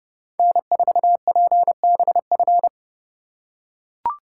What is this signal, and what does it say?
Morse code, heard as a keyed tone, about 28 wpm, 700 Hz. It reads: N4PBF